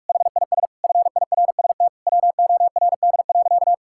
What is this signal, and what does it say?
Morse code, heard as a keyed tone, about 46 wpm, 700 Hz. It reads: HIS FIRST WORD.
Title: Pangram Challenge v2 - 45wpm